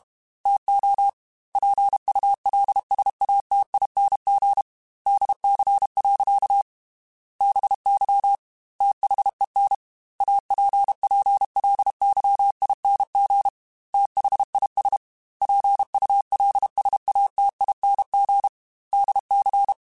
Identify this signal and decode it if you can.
Morse code, heard as a keyed tone, about 32 wpm, 800 Hz. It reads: TO PULSATING DC. BY THEN APPLYING THIS PULSATING DC
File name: Technical Subjects / Technical Subjects-mp3s / Sw Pwr Sup @30